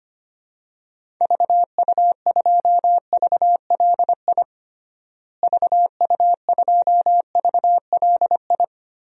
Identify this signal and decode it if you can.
Morse code, heard as a keyed tone, about 25 wpm, 700 Hz. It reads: VU2VLI VU2VLI
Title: DXCC-100 Call Signs - 6 Characters - Repeated 2x - 25wpm